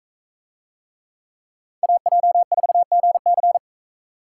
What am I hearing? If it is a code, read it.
Morse code, heard as a keyed tone, about 43 wpm, 700 Hz. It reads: AJ4GC